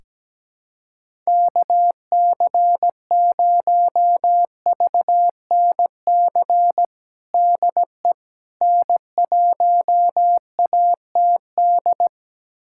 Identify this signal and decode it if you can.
Morse code, heard as a keyed tone, about 17 wpm, 700 Hz. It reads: KC0VNC DE N1ATD
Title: USA Callsign DE Callsign 17wpm